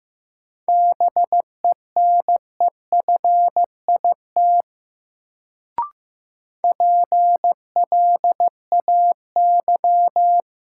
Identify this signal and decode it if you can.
Morse code, heard as a keyed tone, about 15 wpm, 700 Hz. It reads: BENEFIT PLAY
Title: Sets of 5 Words Encore - New Word Drill - 15wpm